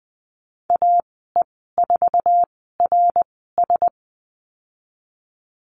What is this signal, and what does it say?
Morse code, heard as a keyed tone, about 20 wpm, spaced as for 15 wpm, 700 Hz. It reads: AE4RS
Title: 069 Lesson 12g - Letter d - Call Signs 20wpm